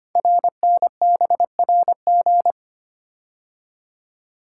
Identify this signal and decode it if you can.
Morse code, heard as a keyed tone, about 25 wpm, 700 Hz. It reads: RNBRG
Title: Code Groups - 5 Letters - 25wpm